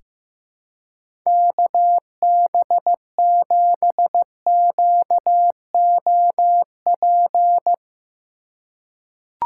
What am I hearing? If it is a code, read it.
Morse code, heard as a keyed tone, about 16 wpm, 700 Hz. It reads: KB7QOP
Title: K1USN SST QSO Practice - 15wpm